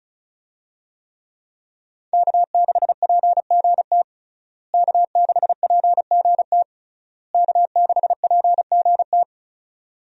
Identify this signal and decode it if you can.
Morse code, heard as a keyed tone, about 36 wpm, 700 Hz. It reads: K6PGT K6PGT K6PGT
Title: USA Call Signs - 5 Characters - Repeated 3x - 35wpm